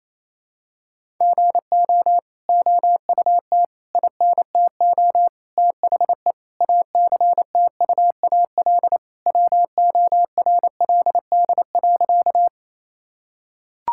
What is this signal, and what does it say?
Morse code, heard as a keyed tone, about 28 wpm, 700 Hz. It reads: GO OUT INTO THE ACTUAL WORLD.
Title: Sentences from Top 1000 Words - Review Part 5 - 28wpm